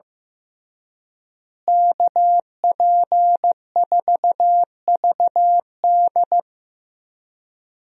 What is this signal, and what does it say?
Morse code, heard as a keyed tone, about 15 wpm, 700 Hz. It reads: KP4VD
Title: DXCC-100 Call Signs - 5 Characters - Repeated 3x - 15wpm